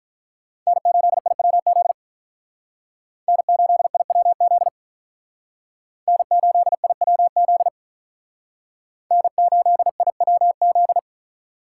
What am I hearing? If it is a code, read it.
Morse code, heard as a keyed tone, about 44 wpm, 700 Hz. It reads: N8IWZ N8IWZ N8IWZ N8IWZ
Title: Speed Racing - USA 1x3 Call Signs 35wpm